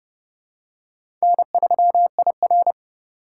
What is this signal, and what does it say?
Morse code, heard as a keyed tone, about 29 wpm, 700 Hz. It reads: N3IR